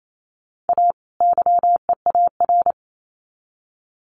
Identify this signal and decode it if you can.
Morse code, heard as a keyed tone, about 28 wpm, 700 Hz. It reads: A YEAR
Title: Sets of 2 Words 28wpm